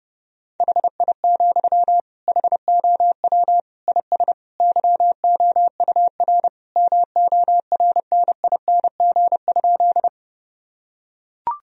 Morse code, heard as a keyed tone, about 30 wpm, 700 Hz. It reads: HI, HOW IS YOUR MORNING?
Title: Sentences from Top 1000 Words - Review Part 3 - 30wpm